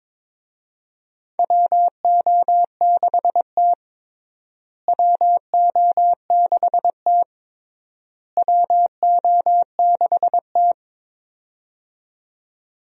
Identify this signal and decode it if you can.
Morse code, heard as a keyed tone, about 21 wpm, 700 Hz. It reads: WO6T WO6T WO6T
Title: USA Call Signs - 4 Characters - Repeated 3x - 22wpm